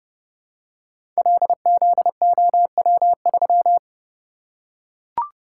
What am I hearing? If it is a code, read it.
Morse code, heard as a keyed tone, about 30 wpm, 700 Hz. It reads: LZOW3
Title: Code Groups - 5 Letter:Numbers - 30wpm